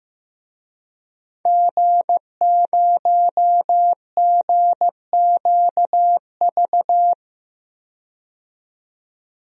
Morse code, heard as a keyed tone, about 15 wpm, 700 Hz. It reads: G0GQV
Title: DXCC-100 Call Signs - 5 Characters - 15wpm